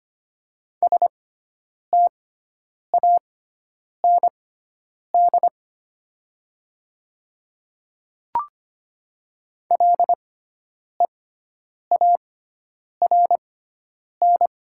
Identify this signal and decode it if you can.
Morse code, heard as a keyed tone, about 25 wpm, spaced as for 9 wpm, 700 Hz. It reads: STAND LEARN
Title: 6x Character Spacing - Top 300 Words - 25wpm